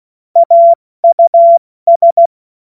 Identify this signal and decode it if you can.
Morse code, heard as a keyed tone, about 16 wpm, spaced as for 14 wpm, 680 Hz. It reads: AUS
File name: Morsekurs / Witze / Witz 02 April 2024 70 F80